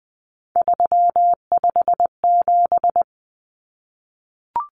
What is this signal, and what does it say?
Morse code, heard as a keyed tone, about 20 wpm, 700 Hz. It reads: 357
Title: Common Cut Numbers - 3-Digit 20wpm